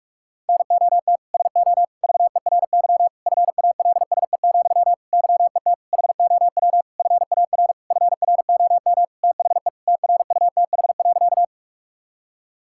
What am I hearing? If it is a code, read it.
Morse code, heard as a keyed tone, about 46 wpm, 700 Hz. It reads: NOT SO VERY FALSE, YET HOW FAR FROM THE TRUTH.
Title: Sentences from Top 1600 Words 45wpm